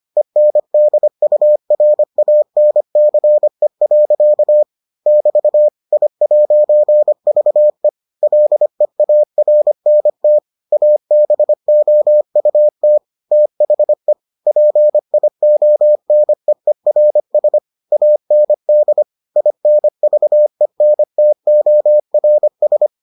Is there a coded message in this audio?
Yes: Morse code from 0.2 to 22.9 s, heard as a keyed tone, about 25 wpm, 600 Hz. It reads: ENDURANCE. <BT> I'VE LEARNT ABOUT THE PIONEERS AND INVENTORS